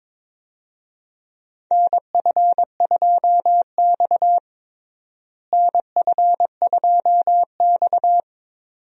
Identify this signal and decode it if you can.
Morse code, heard as a keyed tone, about 21 wpm, 700 Hz. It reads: NF2X NF2X
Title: USA Call Signs - 4 Characters - Repeated 2x - 22wpm